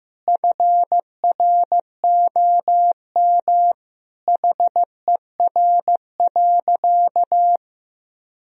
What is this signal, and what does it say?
Morse code, heard as a keyed tone, about 15 wpm, 700 Hz. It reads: FROM HER.
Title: Sentences from Top 1000 Words - Review Part 3 - 15wpm